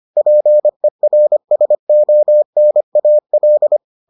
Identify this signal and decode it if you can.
Morse code, heard as a keyed tone, about 25 wpm, 600 Hz. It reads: PERSONAL